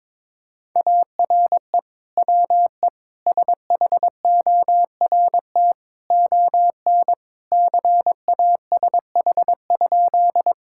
Morse code, heard as a keyed tone, about 22 wpm, 700 Hz. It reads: ARE WE SHORT ON CASH?
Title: Sentences from Top 1000 Words - Review Part 5 - 22wpm